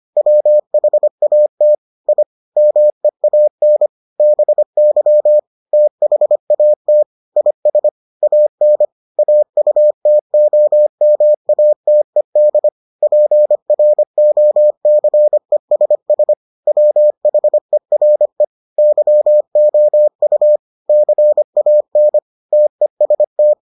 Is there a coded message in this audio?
Yes: Morse code from 0.2 to 23.5 s, heard as a keyed tone, about 25 wpm, 600 Hz. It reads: WHAT I MEAN BY THAT IS AN AUTOMATED PROCESS WHERE YOU CAN TEST